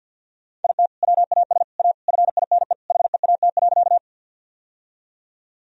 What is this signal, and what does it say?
Morse code, heard as a keyed tone, about 52 wpm, 700 Hz. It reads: IT WAS A FINE HEAT.
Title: Sentences from Top 1000 Words - Review Part 3 - 50wpm